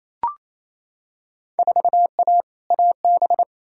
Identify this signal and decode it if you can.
Morse code, heard as a keyed tone, about 28 wpm, 700 Hz. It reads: 4A AB
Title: Field Day Exchanges Encore - 28wpm